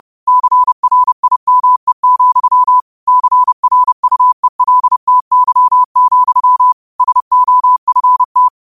Morse code, heard as a keyed tone, about 30 wpm, 1000 Hz. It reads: CRIME, CRUELTY, SOFT
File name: qotdfiles / 30wpm